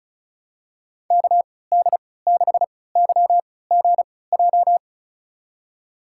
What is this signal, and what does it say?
Morse code, heard as a keyed tone, about 36 wpm, spaced as for 20 wpm, 700 Hz. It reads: KD6YGJ
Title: USA Call Signs - 6 Characters - 3x Character Spacing - 35wpm